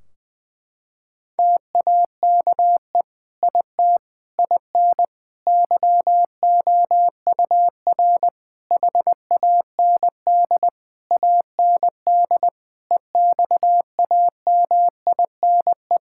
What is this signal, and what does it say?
Morse code, heard as a keyed tone, about 20 wpm, 700 Hz. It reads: TAKE IT IN YOUR HAND AND EXAMINE
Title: Sentences from Top 1600 Words 20wpm